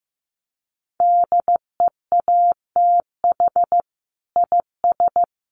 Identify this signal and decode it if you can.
Morse code, heard as a keyed tone, about 15 wpm, 700 Hz. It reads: DEATH IS
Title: Sentences from Top 1500 Words 15wpm